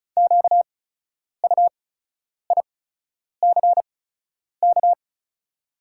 Morse code, heard as a keyed tone, about 36 wpm, spaced as for 10 wpm, 690 Hz. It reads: QUICK
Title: 8x Character Spacing - Top 500 Words - 35wpm